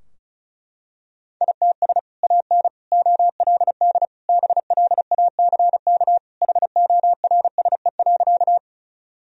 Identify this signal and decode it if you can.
Morse code, heard as a keyed tone, about 36 wpm, 700 Hz. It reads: ITS AN OLD BLACK HORSE.
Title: Sentences from Top 1000 Words - Review Part 2 - 35wpm